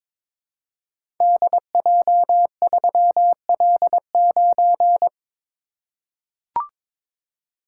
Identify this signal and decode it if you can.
Morse code, heard as a keyed tone, about 22 wpm, 700 Hz. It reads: DJ3L9
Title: Code Groups - 5 Letter:Numbers - 22wpm